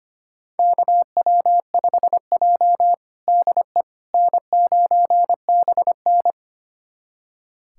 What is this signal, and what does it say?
Morse code, heard as a keyed tone, about 25 wpm, 700 Hz. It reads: KW5J DE N9BN
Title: USA Callsign DE Callsign - 4 Characters - 25wpm